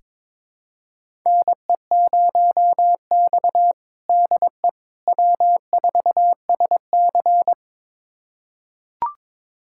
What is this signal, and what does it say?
Morse code, heard as a keyed tone, about 22 wpm, 700 Hz. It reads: NE0X DE W4SC
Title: USA Callsign DE Callsign - 4 Characters - 22wpm